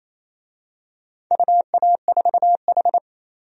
Morse code, heard as a keyed tone, about 28 wpm, 700 Hz. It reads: UA4H
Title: DXCC-100 Call Signs - 4 Characters - 28wpm